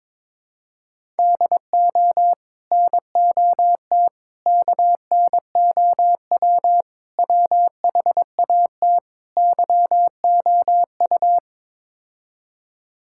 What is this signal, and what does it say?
Morse code, heard as a keyed tone, about 21 wpm, 700 Hz. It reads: DO NOT KNOW WHAT YOU
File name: practice / 22wpm / Build A Phrase - 5-Word Phrases - 22wpm